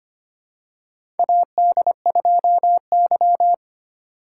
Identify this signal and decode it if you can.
Morse code, heard as a keyed tone, about 25 wpm, 700 Hz. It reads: AD2Y
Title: USA Call Signs - 4 Characters - 4x Character Spacing - 25wpm